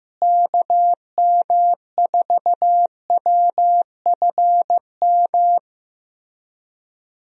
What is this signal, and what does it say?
Morse code, heard as a keyed tone, about 15 wpm, 700 Hz. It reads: KM4WFM
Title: USA Call Signs - 6 Characters - Repeated 3x - 15wpm